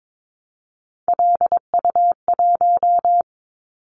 Morse code, heard as a keyed tone, about 22 wpm, 700 Hz. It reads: LU1
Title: Callsign Prefixes for Top 50 DXCC - 22wpm